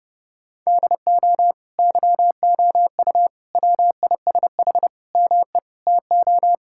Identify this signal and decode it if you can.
Morse code, heard as a keyed tone, about 30 wpm, 700 Hz. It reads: DO YOU WISH ME TO
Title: Sentences from Top 1600 Words 30wpm